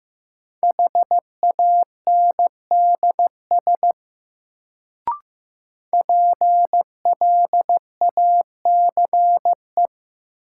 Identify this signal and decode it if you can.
Morse code, heard as a keyed tone, about 16 wpm, 700 Hz. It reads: HANDS PLACE
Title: Sets of 4 Words - New Word Drill - 15wpm